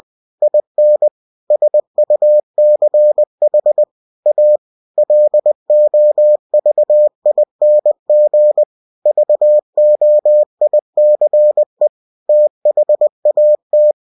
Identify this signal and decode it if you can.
Morse code, heard as a keyed tone, about 20 wpm, 600 Hz. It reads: IN SUCH A LOVING VOICE THAT
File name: cw-books / winnie-the-pooh / winnie-20wpm-page-0017